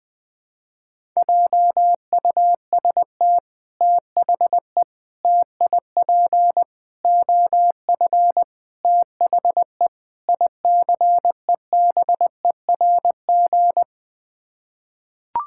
Morse code, heard as a keyed tone, about 20 wpm, 700 Hz. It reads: JUST THE TIP OF THE ICEBERG
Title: Sets of 6 Words - 20wpm